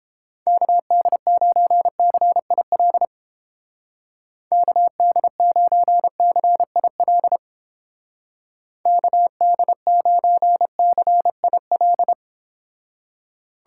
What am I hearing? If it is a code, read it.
Morse code, heard as a keyed tone, about 31 wpm, 700 Hz. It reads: KD9CIL KD9CIL KD9CIL